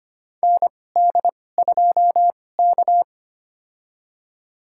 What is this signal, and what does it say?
Morse code, heard as a keyed tone, about 25 wpm, spaced as for 18 wpm, 700 Hz. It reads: ND2K
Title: USA Call Signs - 4 Characters - 2x Character Spacing - 25wpm